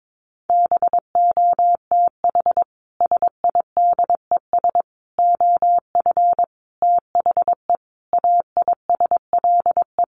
Morse code, heard as a keyed tone, about 22 wpm, 700 Hz. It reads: BOTH SIDES OF THE AISLE
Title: Sets of 5 Words Encore2 22wpm